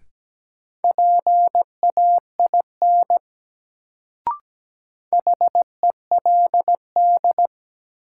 Morse code, heard as a keyed tone, about 17 wpm, 700 Hz. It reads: PAIN HELD
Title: Sentences from Top 1600 Words - New Word Drill - 17wpm